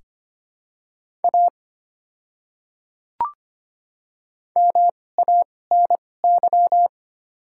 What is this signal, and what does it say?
Morse code, heard as a keyed tone, about 25 wpm, spaced as for 18 wpm, 700 Hz. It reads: A MANY
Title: 2x Character Spacing - Top 100 Words - 25wpm